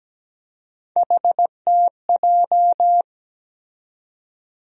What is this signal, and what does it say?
Morse code, heard as a keyed tone, about 17 wpm, 700 Hz. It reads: HTJ
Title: USA Call Sign Postfixes - 3 Letters 17wpm